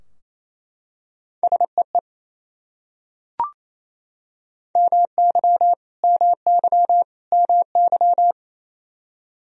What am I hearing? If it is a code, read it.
Morse code, heard as a keyed tone, about 28 wpm, 700 Hz. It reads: SEE MY MY MY
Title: Top 100 Words Repeated 28wpm